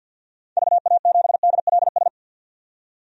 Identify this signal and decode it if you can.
Morse code, heard as a keyed tone, about 47 wpm, 700 Hz. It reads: VA7DLS
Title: DXCC-100 Call Signs - 6 Characters - 50wpm